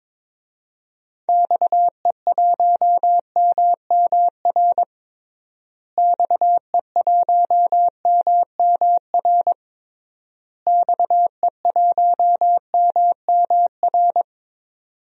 Morse code, heard as a keyed tone, about 22 wpm, 700 Hz. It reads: XE1MMR XE1MMR XE1MMR
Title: DXCC-100 Call Signs - 6 Characters - Repeated 3x - 22wpm